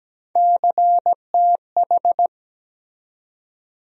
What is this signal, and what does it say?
Morse code, heard as a keyed tone, about 17 wpm, 700 Hz. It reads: CTH
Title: USA Call Sign Postfixes - 3 Letters 17wpm